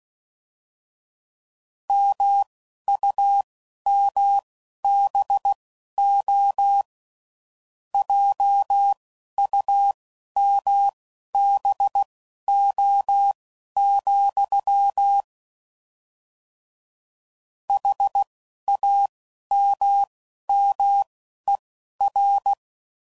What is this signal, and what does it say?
Morse code, heard as a keyed tone, about 16 wpm, spaced as for 12 wpm, 800 Hz. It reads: MUMBO JUMBO, HAMMER
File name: Mixed / Misc / WordPairs